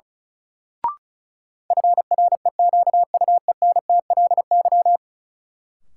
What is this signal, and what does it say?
Morse code, heard as a keyed tone, about 35 wpm, 700 Hz. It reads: FREQUENTLY
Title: Sentences from Top 1300 Words - New Word Drill - 35wpm